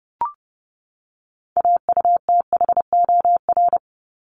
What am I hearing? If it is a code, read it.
Morse code, heard as a keyed tone, about 30 wpm, 700 Hz. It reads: AUTHOR
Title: Top 1001-1250 Words 30wpm